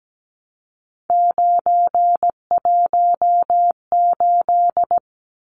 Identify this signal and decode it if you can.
Morse code, heard as a keyed tone, about 17 wpm, 700 Hz. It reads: 918
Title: Numbers - 3-Digit 17wpm